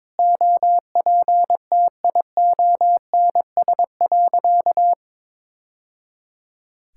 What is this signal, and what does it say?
Morse code, heard as a keyed tone, about 22 wpm, 700 Hz. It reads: OPTIONS.
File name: practice / 22wpm / Sentences with 3-Word Phrases - 22wpm